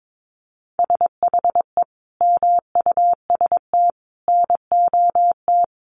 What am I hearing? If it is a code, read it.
Morse code, heard as a keyed tone, about 22 wpm, 700 Hz. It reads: SHE MUST NOT
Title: Sentences from Top 1500 Words 22wpm